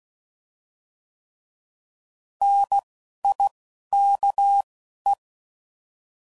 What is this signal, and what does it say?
Morse code, heard as a keyed tone, about 16 wpm, spaced as for 11 wpm, 800 Hz. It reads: NIKE